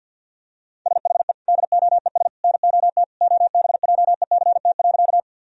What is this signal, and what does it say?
Morse code, heard as a keyed tone, about 49 wpm, 700 Hz. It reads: SHE DOES NOT OBJECT.
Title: Sentences from Top 1000 Words - Review Part 3 - 50wpm